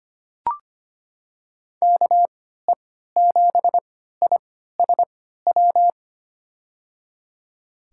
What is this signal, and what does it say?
Morse code, heard as a keyed tone, about 25 wpm, spaced as for 14 wpm, 700 Hz. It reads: KE7ISW
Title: USA Call Signs - 3x Character Spacing - 25wpm